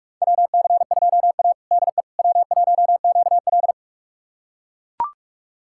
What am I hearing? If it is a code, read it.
Morse code, heard as a keyed tone, about 44 wpm, 700 Hz. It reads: WC2A DE W1QL
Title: USA Callsign DE Callsign - 4 Characters - 45wpm